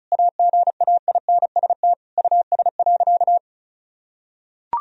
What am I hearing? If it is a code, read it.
Morse code, heard as a keyed tone, about 34 wpm, 700 Hz. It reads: AGAINST US.
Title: Sentences from Top 1000 Words - Review Part 2 - 35wpm